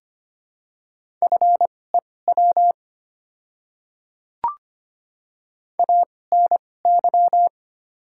Morse code, heard as a keyed tone, about 25 wpm, spaced as for 18 wpm, 700 Hz. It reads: FEW ANY
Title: 2x Character Spacing - Top 300 Words - 25wpm